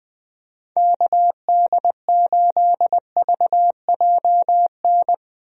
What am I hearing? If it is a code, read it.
Morse code, heard as a keyed tone, about 20 wpm, 700 Hz. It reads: KD8VJN
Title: USA Call Signs - 6 Characters - 20wpm